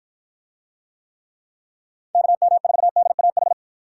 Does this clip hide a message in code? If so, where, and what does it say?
Yes: Morse code from 2.1 to 3.5 s, heard as a keyed tone, about 53 wpm, 700 Hz. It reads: KM4DAH